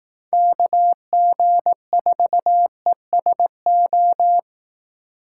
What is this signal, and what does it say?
Morse code, heard as a keyed tone, about 18 wpm, 700 Hz. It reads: KG4ESO
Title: Speed Racing - USA 2x3 Call Signs 15wpm